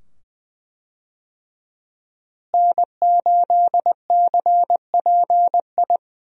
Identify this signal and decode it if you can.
Morse code, heard as a keyed tone, about 20 wpm, 700 Hz. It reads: N8CPI
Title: USA Call Signs - 5 Characters - Repeated 2x - 20wpm